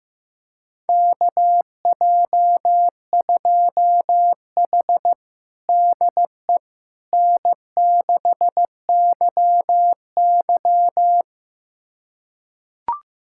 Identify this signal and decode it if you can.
Morse code, heard as a keyed tone, about 15 wpm, 700 Hz. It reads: KJ2H DE N6YY
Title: USA Callsign DE Callsign - 4 Characters - 15wpm